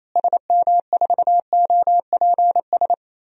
Morse code, heard as a keyed tone, about 28 wpm, 700 Hz. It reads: SM4OPS
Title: DXCC-100 Call Signs - 6 Characters - Repeated 3x - 28wpm